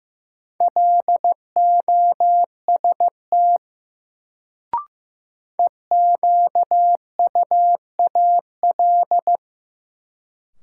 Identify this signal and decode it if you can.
Morse code, heard as a keyed tone, about 15 wpm, 700 Hz. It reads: LOST EQUAL
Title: Sentences from Top 1400 Words - New Word Drill - 15wpm